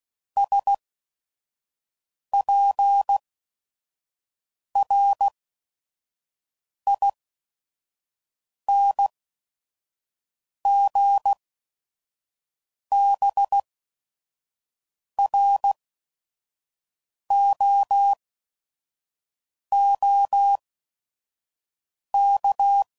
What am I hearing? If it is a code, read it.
Morse code, heard as a keyed tone, about 16 wpm, spaced as for 5 wpm, 800 Hz. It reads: SPRINGBROOK
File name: CW Code Groups @5WPM / fromSM's / Cities